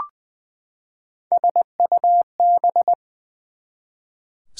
Morse code, heard as a keyed tone, about 20 wpm, 700 Hz. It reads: SUB
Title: Prefixes and Suffixes 20wpm